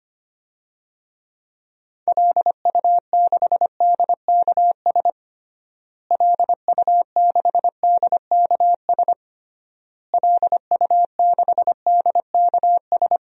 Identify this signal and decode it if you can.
Morse code, heard as a keyed tone, about 25 wpm, 700 Hz. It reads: LU6DKS LU6DKS LU6DKS
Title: DXCC-100 Call Signs - 6 Characters - Repeated 3x - 25wpm